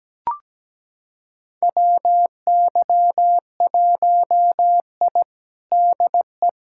Morse code, heard as a keyed tone, about 17 wpm, 700 Hz. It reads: WY1I DE
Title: USA Callsign DE Callsign - 4 Characters - 17wpm